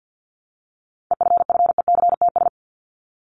Morse code, heard as a keyed tone, about 50 wpm, 700 Hz. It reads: EFFECTS
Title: ES:S Suffix - Top 100 Words with Base Words 50wpm